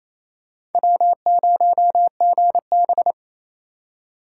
Morse code, heard as a keyed tone, about 28 wpm, 700 Hz. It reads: W0GB
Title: USA Call Signs - 6x Character Spacing - 28wpm